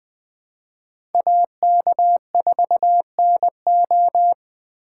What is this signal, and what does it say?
Morse code, heard as a keyed tone, about 21 wpm, 700 Hz. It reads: AK4NO
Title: Speed Racing - USA 2x2 Call Signs 17wpm